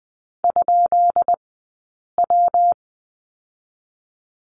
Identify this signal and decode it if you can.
Morse code, heard as a keyed tone, about 20 wpm, 700 Hz. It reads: ? W
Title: 132 Lesson 22d - Question Mark - Two Characters 20wpm